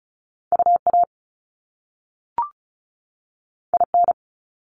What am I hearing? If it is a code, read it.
Morse code, heard as a keyed tone, about 36 wpm, 700 Hz. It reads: UA IN
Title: Top Two Letter Combinations in Words 35wpm